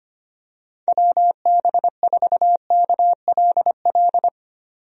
Morse code, heard as a keyed tone, about 25 wpm, 700 Hz. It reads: WB4KLL
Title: USA Call Signs - 6 Characters - 25wpm